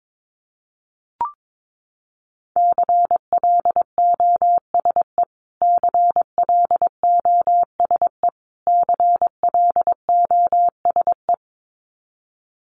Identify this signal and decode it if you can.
Morse code, heard as a keyed tone, about 22 wpm, 700 Hz. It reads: CLOSE CLOSE CLOSE
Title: Top 300 Words Repeated 22wpm